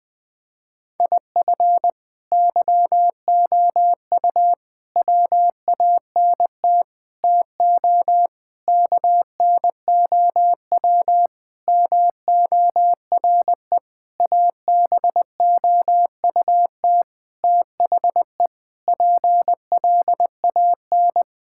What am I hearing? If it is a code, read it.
Morse code, heard as a keyed tone, about 20 wpm, 700 Hz. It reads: IF YOU WANT TO KNOW MORE ABOUT THE PLAN
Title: Sentences with 8-Word Phrases - 20wpm